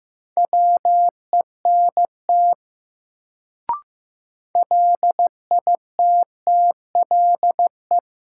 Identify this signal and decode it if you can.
Morse code, heard as a keyed tone, about 15 wpm, 700 Hz. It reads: WENT LITTLE
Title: Sets of 2 Words - New Word Drill - 15wpm